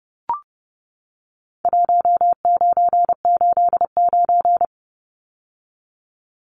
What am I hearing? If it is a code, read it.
Morse code, heard as a keyed tone, about 30 wpm, 700 Hz. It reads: 1989
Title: Numbers - 4-Digit 30wpm